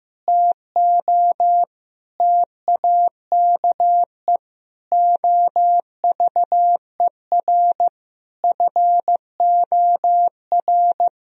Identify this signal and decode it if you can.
Morse code, heard as a keyed tone, about 14 wpm, 700 Hz. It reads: TO TAKE OVER FOR
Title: Sentences from Top 1000 Words - Review Part 3 - 15wpm